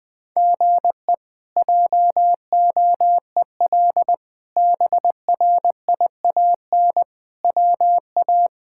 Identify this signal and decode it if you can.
Morse code, heard as a keyed tone, about 20 wpm, 700 Hz. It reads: GE JOEL BRIAN WA
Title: K1USN SST QSO Practice - 20wpm